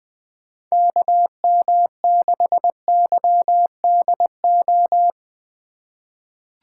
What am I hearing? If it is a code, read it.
Morse code, heard as a keyed tone, about 20 wpm, 700 Hz. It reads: KM6YDO
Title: USA Call Signs - 6 Characters - 2x Character Spacing - 20wpm